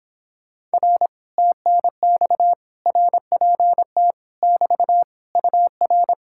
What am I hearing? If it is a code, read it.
Morse code, heard as a keyed tone, about 26 wpm, 700 Hz. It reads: R TNX RPT <BT> UR